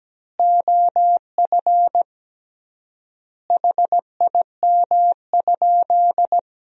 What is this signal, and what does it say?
Morse code, heard as a keyed tone, about 17 wpm, 700 Hz. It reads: OF HIM?
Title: 3x Word-Spacing - Sentences from Top 100 Words - 17wpm